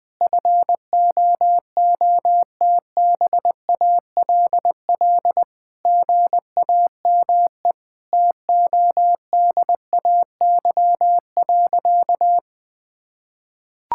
Morse code, heard as a keyed tone, about 20 wpm, 700 Hz. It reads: FOOTBALL GAME TODAY.
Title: Sentences from Top 1600 Words 20wpm